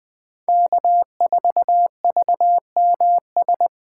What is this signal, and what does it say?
Morse code, heard as a keyed tone, about 20 wpm, 700 Hz. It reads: K4VMS
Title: USA Call Signs - 5 Characters - 4x Character Spacing - 20wpm